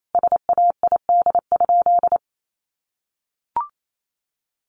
Morse code, heard as a keyed tone, about 28 wpm, 700 Hz. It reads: SAID?
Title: Sentences from Top 100 Words 28wpm